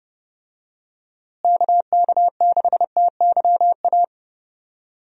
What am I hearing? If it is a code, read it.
Morse code, heard as a keyed tone, about 30 wpm, 700 Hz. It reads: KK6TYA
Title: USA Call Signs - 6 Characters - Repeated 2x - 30wpm